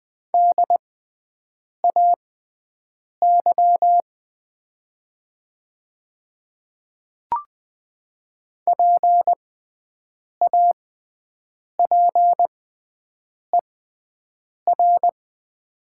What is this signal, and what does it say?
Morse code, heard as a keyed tone, about 20 wpm, spaced as for 7 wpm, 700 Hz. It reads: DAY PAPER
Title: 6x Character Spacing - Top 300 Words - 20wpm